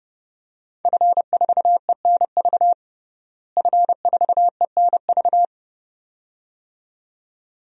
Morse code, heard as a keyed tone, about 30 wpm, 700 Hz. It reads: F4ENV F4ENV
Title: DXCC-100 Call Signs - 5 Characters - Repeated 2x - 30wpm